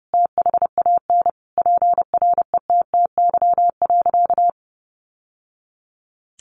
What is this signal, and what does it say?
Morse code, heard as a keyed tone, about 30 wpm, 700 Hz. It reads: THAN PRETTY.
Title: Sentences from Top 1000 Words 30wpm